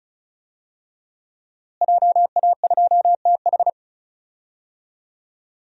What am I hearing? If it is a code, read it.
Morse code, heard as a keyed tone, about 34 wpm, 700 Hz. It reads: JA2TH
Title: DXCC-100 Call Signs - 5 Characters - Repeated 3x - 35wpm